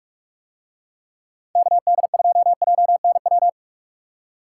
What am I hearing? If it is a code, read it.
Morse code, heard as a keyed tone, about 41 wpm, 700 Hz. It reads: KD2JNW